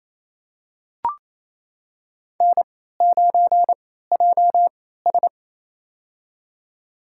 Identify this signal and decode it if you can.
Morse code, heard as a keyed tone, about 28 wpm, spaced as for 16 wpm, 700 Hz. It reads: N9JS
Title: USA Call Signs - 4 Characters - 3x Character Spacing - 28wpm